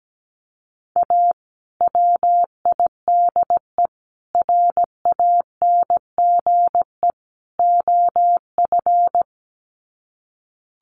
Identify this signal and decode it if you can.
Morse code, heard as a keyed tone, about 17 wpm, 700 Hz. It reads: A WIDE RANGE OF
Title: Sets of 4 Words 17wpm